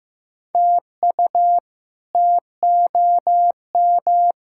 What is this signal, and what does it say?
Morse code, heard as a keyed tone, about 15 wpm, 700 Hz. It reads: TU TOM
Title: K1USN SST QSO Practice - 15wpm